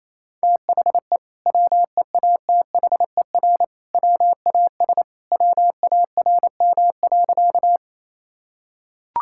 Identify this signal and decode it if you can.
Morse code, heard as a keyed tone, about 27 wpm, 700 Hz. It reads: THE WEATHER WAS WARM.
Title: Sentences from Top 1000 Words - Review Part 5 - 28wpm